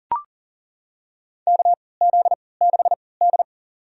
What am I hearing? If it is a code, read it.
Morse code, heard as a keyed tone, about 40 wpm, spaced as for 23 wpm, 700 Hz. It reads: KZ6D
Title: USA Call Signs - 4 Characters - 3x Character Spacing - 40wpm